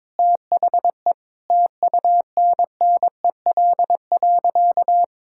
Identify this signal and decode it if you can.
Morse code, heard as a keyed tone, about 22 wpm, 700 Hz. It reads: THE TUNNEL.
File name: practice / 22wpm / Sentences with 8-Word Phrases - 22wpm